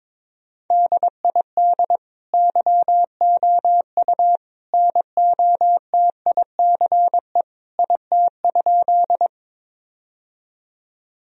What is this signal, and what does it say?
Morse code, heard as a keyed tone, about 22 wpm, 700 Hz. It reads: DID YOU NOTICE IT?
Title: Sentences from Top 1000 Words - Review Part 2 - 22wpm